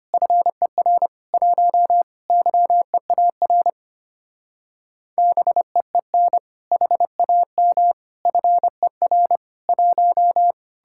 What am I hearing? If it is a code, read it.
Morse code, heard as a keyed tone, about 33 wpm, 700 Hz. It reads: FER 1 YEAR BEEN HAM FER 1